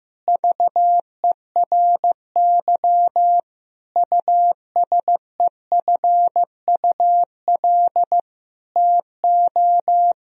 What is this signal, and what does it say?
Morse code, heard as a keyed tone, about 15 wpm, 700 Hz. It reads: VERY USEFUL TO
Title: Sentences from Top 1000 Words - Review Part 4 - 15wpm